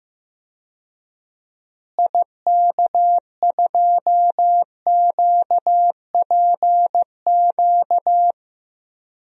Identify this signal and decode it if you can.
Morse code, heard as a keyed tone, about 15 wpm, 700 Hz. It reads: IK2QPQ
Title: DXCC-100 Call Signs - 6 Characters - Repeated 2x - 15wpm